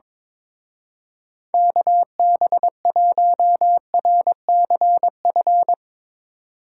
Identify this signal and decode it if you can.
Morse code, heard as a keyed tone, about 22 wpm, 700 Hz. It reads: KB1RCF
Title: Speed Racing - USA 2x3 Call Signs 22wpm